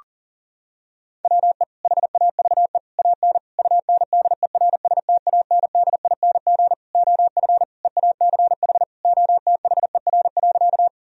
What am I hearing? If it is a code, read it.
Morse code, heard as a keyed tone, about 37 wpm, 700 Hz. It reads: WE HAVE AN UNDERSTANDING OF EACH OTHER.
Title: Sentences from Top 1000 Words - Review Part 3 - 40wpm